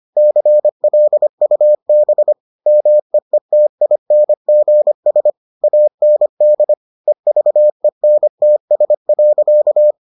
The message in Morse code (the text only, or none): CLUB MEETINGS AND EVENTS.